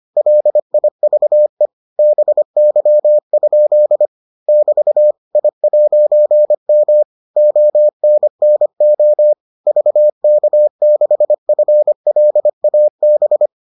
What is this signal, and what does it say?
Morse code, heard as a keyed tone, about 25 wpm, 600 Hz. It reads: LIVE BY? <BT> I'M ONNO VK6FLAB